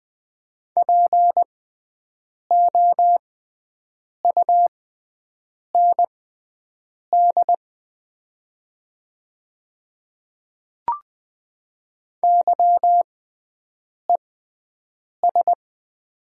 Morse code, heard as a keyed tone, about 20 wpm, spaced as for 7 wpm, 700 Hz. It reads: POUND YES
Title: 6x Character Spacing - Top 500 Words - 20wpm